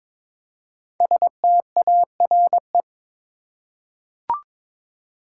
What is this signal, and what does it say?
Morse code, heard as a keyed tone, about 22 wpm, 700 Hz. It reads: STARE
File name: 5 Letter Words - First 3 Letters Given - 22wpm